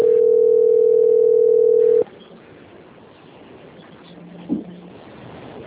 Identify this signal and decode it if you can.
Ringback tone (North America), 440 + 480 Hz.
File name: out- - - - - .467